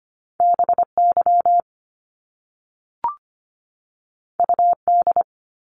Morse code, heard as a keyed tone, about 25 wpm, 700 Hz. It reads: BY UD
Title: Top Two Letter Combinations in Words 25wpm